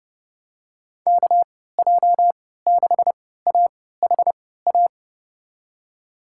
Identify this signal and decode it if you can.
Morse code, heard as a keyed tone, about 30 wpm, spaced as for 17 wpm, 700 Hz. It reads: KJ6AHA
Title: USA Call Signs - 6 Characters - 3x Character Spacing - 30wpm